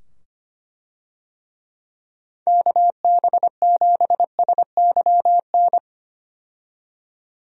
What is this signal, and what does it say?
Morse code, heard as a keyed tone, about 25 wpm, 700 Hz. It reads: KB7SYN
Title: USA Call Signs - 6 Characters - 25wpm